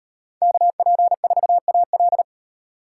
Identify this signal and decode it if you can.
Morse code, heard as a keyed tone, about 36 wpm, 700 Hz. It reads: KP4AL